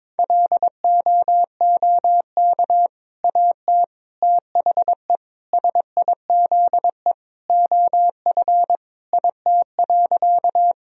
Morse code, heard as a keyed tone, about 22 wpm, 700 Hz. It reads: LOOK AT THE SIZE OF IT.
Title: Sentences from Top 1000 Words - Review Part 2 - 22wpm